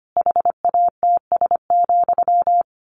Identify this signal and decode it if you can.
Morse code, heard as a keyed tone, about 25 wpm, 700 Hz. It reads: HATS,
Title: A Sound Of Thunder 25wpm